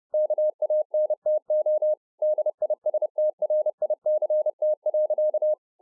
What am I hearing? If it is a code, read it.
Morse code, heard as a keyed tone, about 30 wpm, 610 Hz. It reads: KANTO DISTRICT.